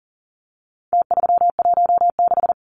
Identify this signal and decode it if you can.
Morse code, heard as a keyed tone, about 40 wpm, 700 Hz. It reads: T316
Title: Numbers - Contest Style - 4-Digit 40wpm